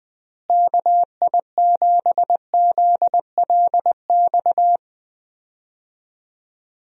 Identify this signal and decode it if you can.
Morse code, heard as a keyed tone, about 20 wpm, 700 Hz. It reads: KI7ZLX
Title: USA Call Signs - 6 Characters - Repeated 2x - 20wpm